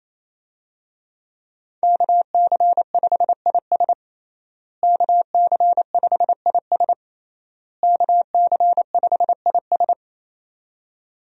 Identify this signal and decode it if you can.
Morse code, heard as a keyed tone, about 27 wpm, 700 Hz. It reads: KC5IS KC5IS KC5IS
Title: USA Call Signs - 5 Characters - Repeated 3x - 28wpm